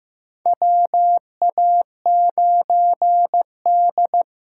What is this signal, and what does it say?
Morse code, heard as a keyed tone, about 15 wpm, 700 Hz. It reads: WA9D
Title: USA Call Signs - 4 Characters - 8x Character Spacing - 15wpm